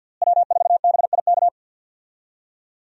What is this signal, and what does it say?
Morse code, heard as a keyed tone, about 52 wpm, 700 Hz. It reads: W4BIK
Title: USA Call Signs - 5 Characters - Repeated 2x - 50wpm